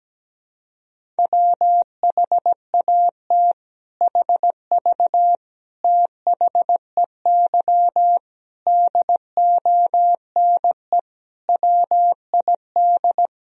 Morse code, heard as a keyed tone, about 17 wpm, 700 Hz. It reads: WHAT HV THEY DONE WID
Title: Sentences from Top 500 Words and Abbreviations 17wpm